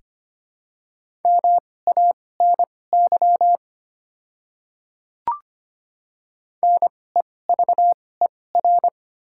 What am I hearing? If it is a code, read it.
Morse code, heard as a keyed tone, about 25 wpm, spaced as for 18 wpm, 700 Hz. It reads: MANY NEVER
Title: 2x Character Spacing - Top 300 Words - 25wpm